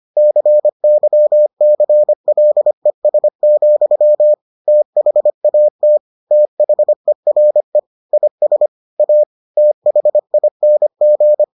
Morse code, heard as a keyed tone, about 25 wpm, 600 Hz. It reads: CYCLES, THAT THERE IS A THING